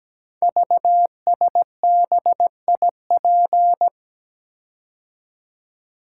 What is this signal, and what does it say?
Morse code, heard as a keyed tone, about 17 wpm, 700 Hz. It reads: VSBIP
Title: Code Groups - 5 Letter:Numbers - 17wpm